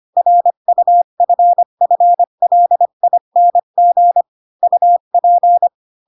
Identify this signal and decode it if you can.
Morse code, heard as a keyed tone, about 25 wpm, 700 Hz. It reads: RUFFLING UP
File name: Pinocchio / Pinocchio ch1 20 25 ik1hge